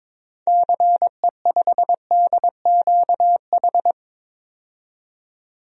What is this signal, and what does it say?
Morse code, heard as a keyed tone, about 22 wpm, 700 Hz. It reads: CE5DQH
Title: DXCC-100 Call Signs - 6 Characters - 22wpm